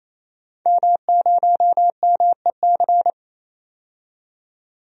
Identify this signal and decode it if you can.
Morse code, heard as a keyed tone, about 28 wpm, 700 Hz. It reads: M0MEC
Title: DXCC-100 Call Signs - 5 Characters - 28wpm